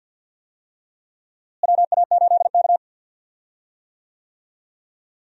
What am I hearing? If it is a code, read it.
Morse code, heard as a keyed tone, about 46 wpm, 700 Hz. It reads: WA8K
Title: USA Call Signs - 4 Characters - Repeated 2x - 50wpm